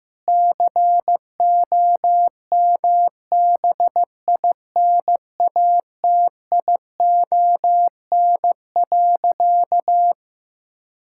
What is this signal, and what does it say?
Morse code, heard as a keyed tone, about 16 wpm, 700 Hz. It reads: COMBINATION.
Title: Sentences from Top 1000 Words - Review Part 6 - 15wpm